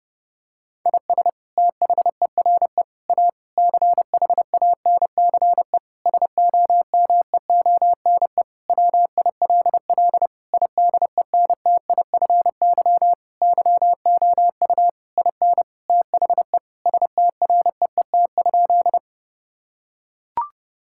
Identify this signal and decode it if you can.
Morse code, heard as a keyed tone, about 30 wpm, 700 Hz. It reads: IS THERE A CHANCE SOMEONE WILL IDENTIFY YOU IN THE STREET?
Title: Sentences from Top 1000 Words - Review Part 6 - 30wpm